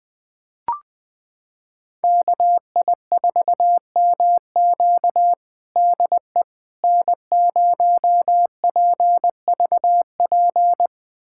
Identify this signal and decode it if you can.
Morse code, heard as a keyed tone, about 20 wpm, 700 Hz. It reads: KI4MQ DE N0PVP
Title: USA Callsign DE Callsign - 5 Characters - 20wpm